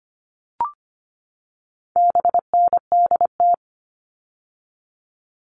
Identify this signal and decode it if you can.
Morse code, heard as a keyed tone, about 25 wpm, 700 Hz. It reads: BNDT
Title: All Cut Numbers - 4-Digit 25wpm